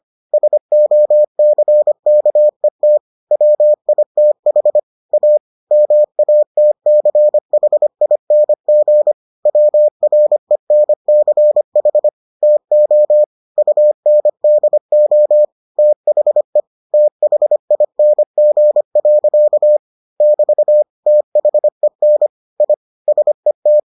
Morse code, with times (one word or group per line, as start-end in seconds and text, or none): 0.3-3.0 SOCKET
3.3-4.8 WITH
5.1-5.4 A
5.7-9.1 MATCHING
9.5-12.1 WRENCH
12.4-13.2 TO
13.6-15.4 UNDO
15.8-16.6 THE
16.9-19.8 THING.
20.2-20.8 <BT>
21.1-22.3 THEN
22.6-22.7 I
23.1-23.8 SET